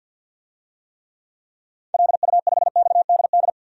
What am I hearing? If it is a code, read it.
Morse code, heard as a keyed tone, about 46 wpm, 700 Hz. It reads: LU5XDD
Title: DXCC-100 Call Signs - 6 Characters - 50wpm